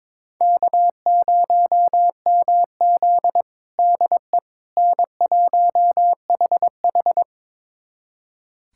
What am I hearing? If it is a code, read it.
Morse code, heard as a keyed tone, about 22 wpm, 700 Hz. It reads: K0MZ DE N1HH
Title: USA Callsign DE Callsign - 4 Characters - 22wpm